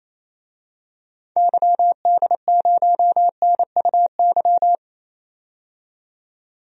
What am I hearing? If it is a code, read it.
Morse code, heard as a keyed tone, about 27 wpm, 700 Hz. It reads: YD0NUY
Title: DXCC-100 Call Signs - 6 Characters - 28wpm